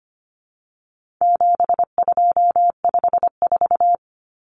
Morse code, heard as a keyed tone, about 25 wpm, 700 Hz. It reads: 7254
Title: Numbers - 4-Digit 25wpm